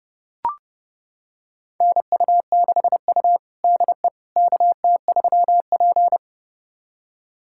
Morse code, heard as a keyed tone, about 32 wpm, 700 Hz. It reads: NU6U DE KT3P
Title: USA Callsign DE Callsign - 4 Characters - 30wpm